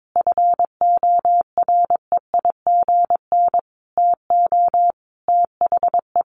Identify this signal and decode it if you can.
Morse code, heard as a keyed tone, about 22 wpm, 700 Hz. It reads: FOREIGN TO THE
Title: Sentences from Top 1400 Words 22wpm